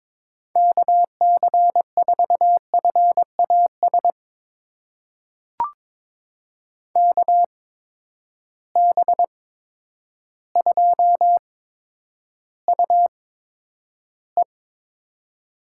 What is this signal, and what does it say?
Morse code, heard as a keyed tone, about 21 wpm, spaced as for 10 wpm, 700 Hz. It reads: KC4FAS KB2UE
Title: USA Call Signs - 6 Characters - 8x Character Spacing - 22wpm